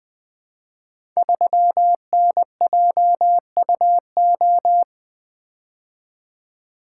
Morse code, heard as a keyed tone, about 20 wpm, 700 Hz. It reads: 3NJUO
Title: Code Groups - 5 Letter:Numbers - 20wpm